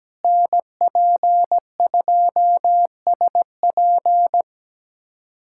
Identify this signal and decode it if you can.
Morse code, heard as a keyed tone, about 17 wpm, 700 Hz. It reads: NP2SP
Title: USA Callsign DE Callsign - 5 Characters - 17wpm